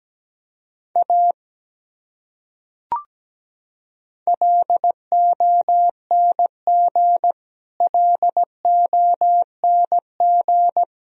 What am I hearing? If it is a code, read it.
Morse code, heard as a keyed tone, about 17 wpm, 700 Hz. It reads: A LONG LONG
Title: Top 100 Words Repeated 17wpm